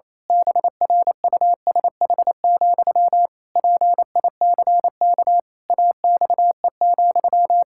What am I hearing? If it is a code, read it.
Morse code, heard as a keyed tone, about 28 wpm, 700 Hz. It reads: BRUSH, PICK AXE,
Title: Pangram Challenge v2 - 28wpm